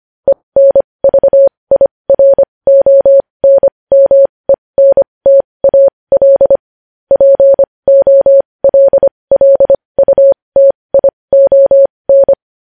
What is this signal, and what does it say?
Morse code, heard as a keyed tone, about 25 wpm, spaced as for 20 wpm, 550 Hz. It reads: ENVIRONMENTAL POLLUTION
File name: podcast / News headlines-1x-25x20